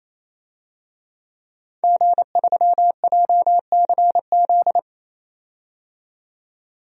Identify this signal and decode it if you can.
Morse code, heard as a keyed tone, about 28 wpm, 700 Hz. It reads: G3JCZ
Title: DXCC-100 Call Signs - 5 Characters - 28wpm